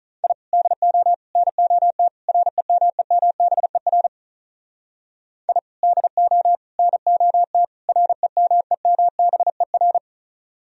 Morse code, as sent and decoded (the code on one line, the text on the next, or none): .. / -.. --- / -. --- - / .-. -- . -- -... .-. / .. / -.. --- / -. --- - / .-. -- . -- -... .-.
I DO NOT REMEMBER I DO NOT REMEMBER